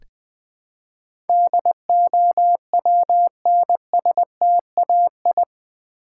Morse code, heard as a keyed tone, about 19 wpm, 700 Hz. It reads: DOWNSTAI
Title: Build A Word - 4 Letters At A Time - 10-Letter Words - 20wpm